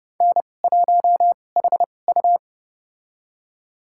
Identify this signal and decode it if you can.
Morse code, heard as a keyed tone, about 30 wpm, spaced as for 22 wpm, 700 Hz. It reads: N1HU
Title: USA Call Signs - 4 Characters - 2x Character Spacing - 30wpm